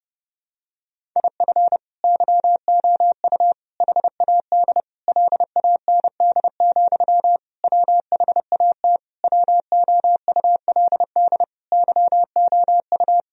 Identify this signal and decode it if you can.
Morse code, heard as a keyed tone, about 32 wpm, 700 Hz. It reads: IF YOU HAD LAND, WHAT WOULD YOU
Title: Sentences from Top 1000 Words - Review Part 1 - 30wpm